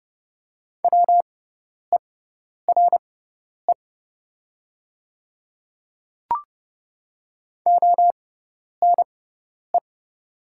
Morse code, heard as a keyed tone, about 30 wpm, spaced as for 10 wpm, 700 Hz. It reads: WERE ONE
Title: 6x Character Spacing - Top 300 Words - 30wpm